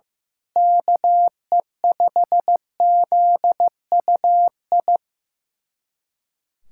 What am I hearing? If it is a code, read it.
Morse code, heard as a keyed tone, about 15 wpm, 700 Hz. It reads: KE5ZUI
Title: USA Callsign DE Callsign 15wpm